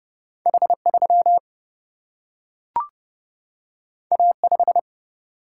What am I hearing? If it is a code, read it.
Morse code, heard as a keyed tone, about 30 wpm, 700 Hz. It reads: H3 A5
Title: Single-Letter Single-Number 30wpm